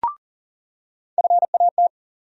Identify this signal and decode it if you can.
Morse code, heard as a keyed tone, about 40 wpm, 700 Hz. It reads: FAT
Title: USA Call Sign Postfixes - 3 Letters 40wpm